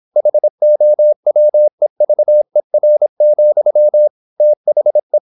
Morse code, heard as a keyed tone, about 26 wpm, 600 Hz. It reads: HOWEVER, THE
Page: Never 26wpm | CW med Gnister